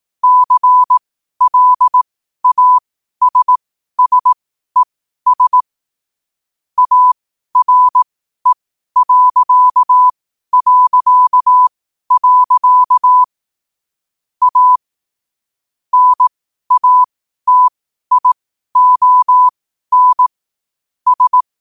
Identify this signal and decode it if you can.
Morse code, heard as a keyed tone, about 18 wpm, spaced as for 12 wpm, 1000 Hz. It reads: CLASSES ARE... A NATION S
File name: qotdfiles / 12wpm